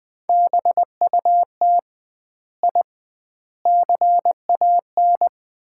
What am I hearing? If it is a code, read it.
Morse code, heard as a keyed tone, about 21 wpm, 700 Hz. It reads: BUT I CAN
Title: 2x Word-Spacing - Sets of 4 Words - 20wpm